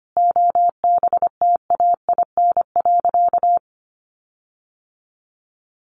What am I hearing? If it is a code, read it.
Morse code, heard as a keyed tone, about 25 wpm, 700 Hz. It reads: OBTAIN.
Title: Sentences from Top 1300 Words 25wpm